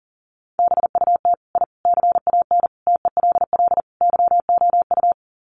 Morse code, heard as a keyed tone, about 37 wpm, 700 Hz. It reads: BUT I CAN TELL YOU
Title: Sets of 5 Words Encore 40wpm